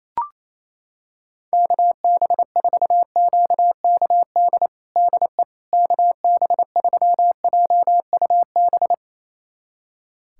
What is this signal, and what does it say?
Morse code, heard as a keyed tone, about 28 wpm, 700 Hz. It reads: KB4QKD DE KB3JUB
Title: USA Callsign DE Callsign - 6 Characters - 28wpm